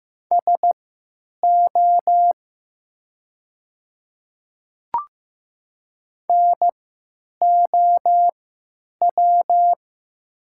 Morse code, heard as a keyed tone, about 15 wpm, spaced as for 9 wpm, 700 Hz. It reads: SO NOW
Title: 3x Character Spacing - Top 100 Words - 15wpm